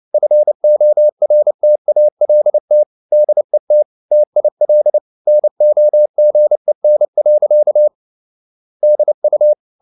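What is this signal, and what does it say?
Morse code, heard as a keyed tone, about 29 wpm, 600 Hz. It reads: FORTALT DET TIL NOGEN. DU
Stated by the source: Fast (QRQ) Morse at 29 wpm, from CW med Gnister.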